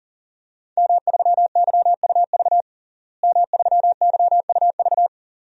Morse code, heard as a keyed tone, about 37 wpm, 690 Hz. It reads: M3YUV M3YUV
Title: DXCC-100 Call Signs - 5 Characters - Repeated 2x - 40wpm